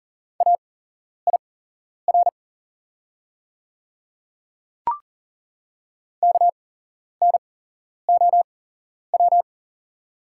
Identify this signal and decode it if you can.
Morse code, heard as a keyed tone, about 37 wpm, spaced as for 11 wpm, 690 Hz. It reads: AIR KNOW
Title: 8x Character Spacing - Top 200 Words - 40wpm